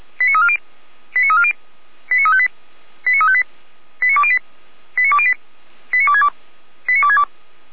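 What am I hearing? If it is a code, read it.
Five-tone selective call (ZVEI), heard as quick runs of five tones, tones of about 80 ms. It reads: 89340 89340 89358 89358 89208 89208 89262 89262
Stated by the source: ZVEI five-tone call received by SDR.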